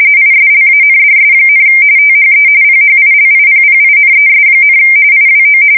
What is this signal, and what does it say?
RTTY, about 45.45 baud. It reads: ISLANDS, THE PHILIPPINES, IWO
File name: fsk170 / battleshipnj